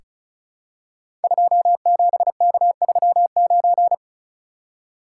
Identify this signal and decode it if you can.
Morse code, heard as a keyed tone, about 36 wpm, 700 Hz. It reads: 27K39
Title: Code Groups - 5 Letter:Numbers - 35wpm